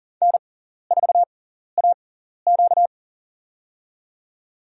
Morse code, heard as a keyed tone, about 37 wpm, spaced as for 14 wpm, 700 Hz. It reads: N4AQ
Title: USA Call Signs - 4 Characters - 6x Character Spacing - 40wpm